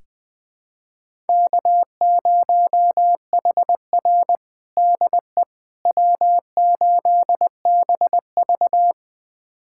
Morse code, heard as a keyed tone, about 20 wpm, 700 Hz. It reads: K0HR DE W8BV
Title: USA Callsign DE Callsign - 4 Characters - 20wpm